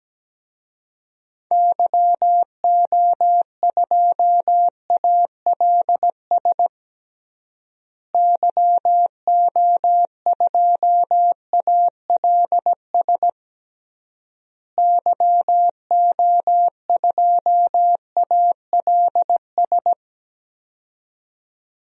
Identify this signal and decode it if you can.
Morse code, heard as a keyed tone, about 17 wpm, 700 Hz. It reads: YO2ALS YO2ALS YO2ALS
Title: DXCC-100 Call Signs - 6 Characters - Repeated 3x - 17wpm